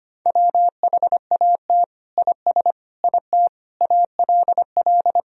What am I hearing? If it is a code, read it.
Morse code, heard as a keyed tone, about 25 wpm, 700 Hz. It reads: WHAT IS IT ALL
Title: Sentences from Top 100 Words and Abbreviations 25wpm